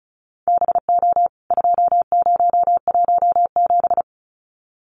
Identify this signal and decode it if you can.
Morse code, heard as a keyed tone, about 34 wpm, 700 Hz. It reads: BO 2017
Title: CWT Exchange 35wpm